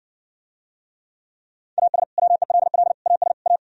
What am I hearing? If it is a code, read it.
Morse code, heard as a keyed tone, about 69 wpm, 700 Hz. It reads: AS WELL AS A